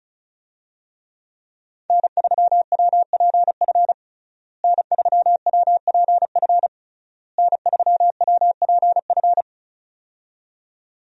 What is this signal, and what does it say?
Morse code, heard as a keyed tone, about 34 wpm, 700 Hz. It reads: N3WPF N3WPF N3WPF
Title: USA Call Signs - 5 Characters - Repeated 3x - 35wpm